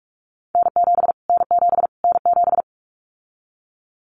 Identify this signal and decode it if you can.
Morse code, heard as a keyed tone, about 44 wpm, 700 Hz. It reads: N7 N7 N7
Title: Callsign Prefixes for Top 25 DXCC - Repeated 3x - 45wpm